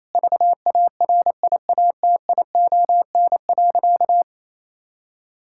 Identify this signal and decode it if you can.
Morse code, heard as a keyed tone, about 28 wpm, 700 Hz. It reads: VARIATION.
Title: Sentences from Top 1600 Words 28wpm